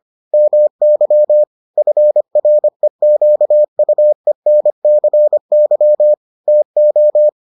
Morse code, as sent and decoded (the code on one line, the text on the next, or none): -- -.-- / ..-. .-. --.- ..- . -. -.-. -.-- / - ---
MY FREQUENCY TO